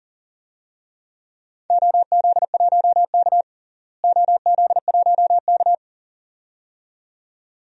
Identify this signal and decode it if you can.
Morse code, heard as a keyed tone, about 40 wpm, 700 Hz. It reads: OZ1K OZ1K
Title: DXCC-100 Call Signs - 4 Characters - Repeated 2x - 40wpm